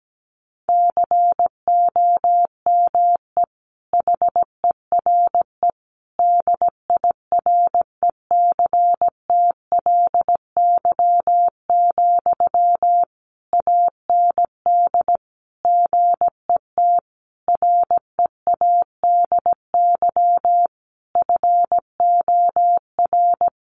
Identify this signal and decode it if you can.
Morse code, heard as a keyed tone, about 17 wpm, 700 Hz. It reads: COME HERE DIRECTLY, AND GET READY FOR
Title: Sentences from Top 800 Words 17wpm